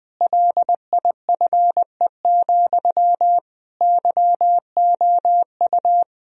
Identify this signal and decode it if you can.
Morse code, heard as a keyed tone, about 20 wpm, 700 Hz. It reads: LIFE, YOU
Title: Sentences from Top 1000 Words - Review Part 3 - 20wpm